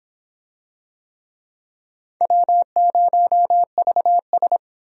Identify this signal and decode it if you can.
Morse code, heard as a keyed tone, about 23 wpm, 700 Hz. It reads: W0VS